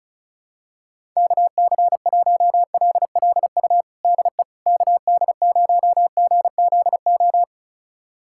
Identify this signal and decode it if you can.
Morse code, heard as a keyed tone, about 36 wpm, 700 Hz. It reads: KC1LLU DE KD0GZO
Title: USA Callsign DE Callsign 35wpm